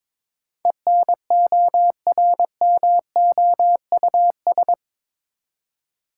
Morse code, heard as a keyed tone, about 22 wpm, 700 Hz. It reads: ENORMOUS
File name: practice / 22wpm / Eight Letter Words - 22wpm